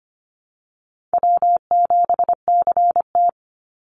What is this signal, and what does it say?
Morse code, heard as a keyed tone, about 25 wpm, 700 Hz. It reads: W7CT
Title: Easy Call Signs - 25wpm